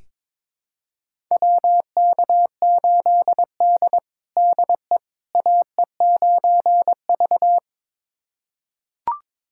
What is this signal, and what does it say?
Morse code, heard as a keyed tone, about 22 wpm, 700 Hz. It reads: WK8D DE AE9V
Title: USA Callsign DE Callsign - 4 Characters - 22wpm